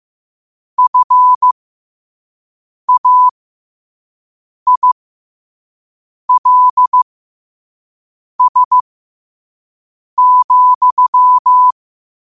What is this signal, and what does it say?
Morse code, heard as a keyed tone, about 15 wpm, spaced as for 5 wpm, 1000 Hz. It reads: FAILS,
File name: qotdfiles / 5wpm